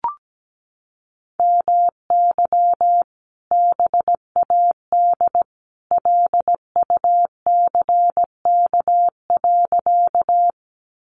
Morse code, heard as a keyed tone, about 17 wpm, 700 Hz. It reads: MY BAD LUCK.
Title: Sentences from Top 1500 Words 17wpm